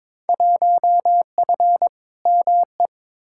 Morse code, heard as a keyed tone, about 22 wpm, 700 Hz. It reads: 1F ME
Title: Field Day Exchanges Encore - 22wpm